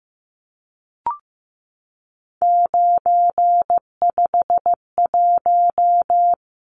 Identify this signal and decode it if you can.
Morse code, heard as a keyed tone, about 15 wpm, 700 Hz. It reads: 951
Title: Numbers - 3-Digit 15wpm